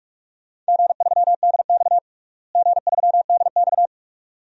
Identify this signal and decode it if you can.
Morse code, heard as a keyed tone, about 46 wpm, 700 Hz. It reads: G3DX G3DX
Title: DXCC-100 Call Signs - 4 Characters - Repeated 2x - 45wpm